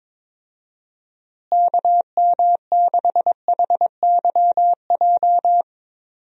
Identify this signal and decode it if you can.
Morse code, heard as a keyed tone, about 21 wpm, 700 Hz. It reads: KM6HYJ
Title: USA Call Signs - 6 Characters - Repeated 3x - 22wpm